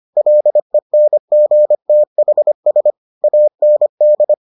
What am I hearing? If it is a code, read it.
Morse code, heard as a keyed tone, about 25 wpm, 600 Hz. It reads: LENGTHS AND